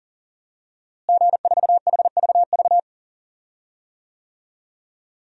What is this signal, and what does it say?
Morse code, heard as a keyed tone, about 40 wpm, 700 Hz. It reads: G4HVV
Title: DXCC-100 Call Signs - 5 Characters - Repeated 3x - 40wpm